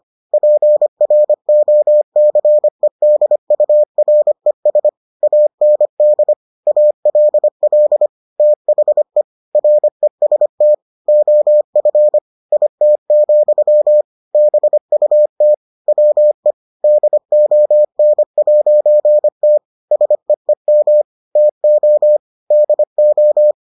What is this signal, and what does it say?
Morse code, heard as a keyed tone, about 25 wpm, 600 Hz. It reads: PROCEDURES AND ALL THE REST OF IT, BUT WE DON'T SEEM TO DO